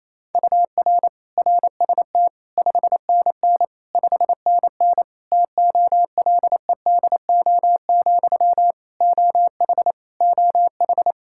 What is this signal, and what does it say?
Morse code, heard as a keyed tone, about 28 wpm, 700 Hz. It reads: UR RST 5NN 5NN TOLEDO, OH OH
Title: Practice QSO - Exchange 1 - Beginning - 28wpm